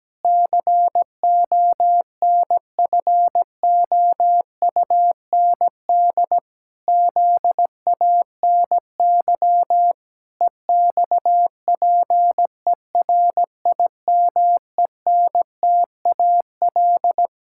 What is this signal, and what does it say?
Morse code, heard as a keyed tone, about 17 wpm, 700 Hz. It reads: CONFOUND ZANY EXPERIMENTAL
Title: Pangram Challenge v2 - 17wpm